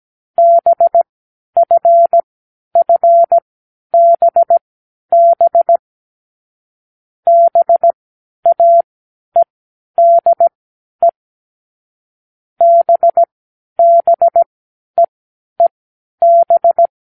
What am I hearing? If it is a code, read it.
Morse code, heard as a keyed tone, about 17 wpm, spaced as for 11 wpm, 680 Hz. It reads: BFFBB BAEDE BBEEB
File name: k7qo / Lesson9(A-F)